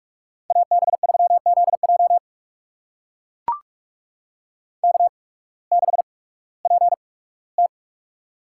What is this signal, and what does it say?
Morse code, heard as a keyed tone, about 44 wpm, spaced as for 21 wpm, 700 Hz. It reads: AB3ZJ K6PT
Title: USA Call Signs - 8x Character Spacing - 45wpm